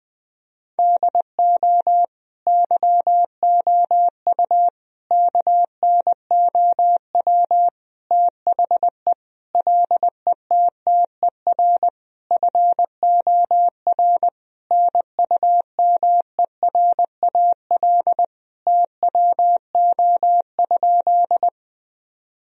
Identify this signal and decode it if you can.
Morse code, heard as a keyed tone, about 20 wpm, 700 Hz. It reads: DO YOU KNOW THE LETTER FOR NUMERAL TWO?
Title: Sentences from Top 1000 Words - Review Part 2 - 20wpm